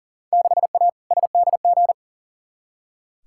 Morse code, heard as a keyed tone, about 40 wpm, 700 Hz. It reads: 6A SDG
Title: Field Day Exchanges Encore - 40wpm